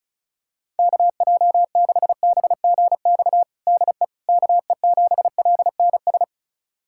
Morse code, heard as a keyed tone, about 34 wpm, 700 Hz. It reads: KJ6BGX DE KE7LNS
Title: USA Callsign DE Callsign - 6 Characters - 35wpm